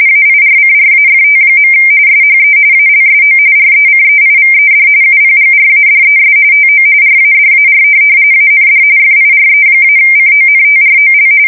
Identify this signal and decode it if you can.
RTTY, about 45.45 baud. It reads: AIRCRAFT. SHE ALSO RECEIVED UPGRADES TO RADAR AND FIRE-CONTROL
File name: fsk170 / USS-Missouri